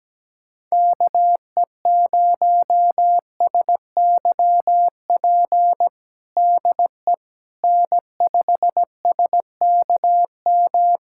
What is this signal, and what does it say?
Morse code, heard as a keyed tone, about 17 wpm, 700 Hz. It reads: KE0SYP DE N5SKM
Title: USA Callsign DE Callsign 17wpm